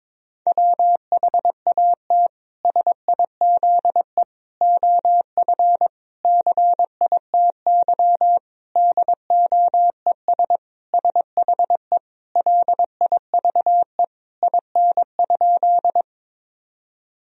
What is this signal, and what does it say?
Morse code, heard as a keyed tone, about 22 wpm, 700 Hz. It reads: WHAT SIZE OF CITY DOES SHE LIVE IN?
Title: Sentences from Top 1000 Words - Review Part 2 - 22wpm